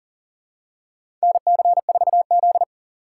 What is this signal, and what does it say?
Morse code, heard as a keyed tone, about 40 wpm, 700 Hz. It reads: NC4Z
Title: USA Call Signs - 4 Characters - 4x Character Spacing - 40wpm